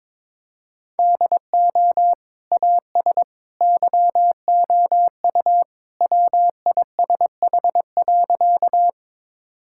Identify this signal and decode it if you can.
Morse code, heard as a keyed tone, about 22 wpm, 700 Hz. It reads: DO AS YOU WISH.
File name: practice / 22wpm / Sentences with 2-Word Phrases - 22wpm